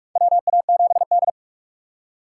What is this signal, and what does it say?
Morse code, heard as a keyed tone, about 46 wpm, 700 Hz. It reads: WA7D
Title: USA Call Signs - 4 Characters - 3x Character Spacing - 45wpm